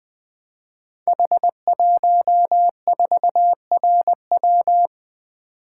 Morse code, heard as a keyed tone, about 20 wpm, 700 Hz. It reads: H14RW
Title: Code Groups - 5 Letter:Numbers - 20wpm